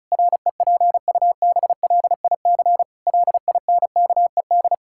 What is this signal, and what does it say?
Morse code, heard as a keyed tone, about 36 wpm, 700 Hz. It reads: REPUBLIC LINKED
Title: Pangram Challenge v2 - 35wpm